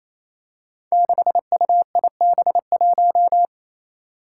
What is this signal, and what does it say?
Morse code, heard as a keyed tone, about 28 wpm, 700 Hz. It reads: 6UIB1
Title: Five Challenging Letter-Number Combinations 28wpm